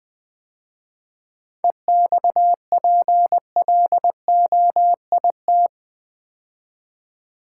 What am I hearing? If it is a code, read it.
Morse code, heard as a keyed tone, about 20 wpm, 700 Hz. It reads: EXPLOIT
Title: 7 Letter Words - First 3 Letters Given - 20wpm